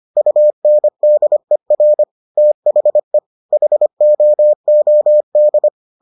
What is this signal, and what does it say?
Morse code, heard as a keyed tone, about 25 wpm, 600 Hz. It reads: UNDER THE HOOD